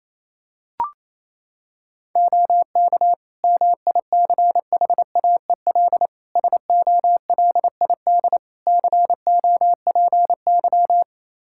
Morse code, heard as a keyed tone, about 27 wpm, 700 Hz. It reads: OK MICHAEL SOLID COPY
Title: Practice QSO - Exchange 3 - More Info - 28wpm